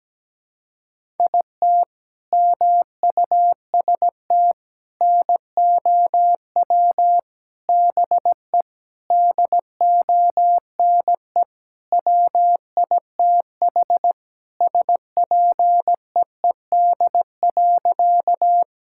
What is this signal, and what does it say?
Morse code, heard as a keyed tone, about 17 wpm, 700 Hz. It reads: IT MUST NOW BE DONE WITH SPEED.
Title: Sentences from Top 1000 Words - Review Part 5 - 17wpm